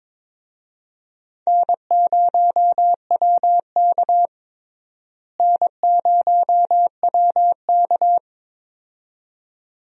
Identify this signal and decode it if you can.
Morse code, heard as a keyed tone, about 22 wpm, 700 Hz. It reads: N0WK N0WK
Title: USA Call Signs - 4 Characters - Repeated 2x - 22wpm